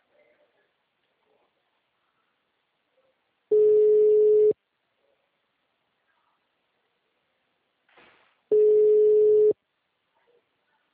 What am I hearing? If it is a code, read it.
Ringback tone (Europe, 425 Hz).
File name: out- -2017- - - .14853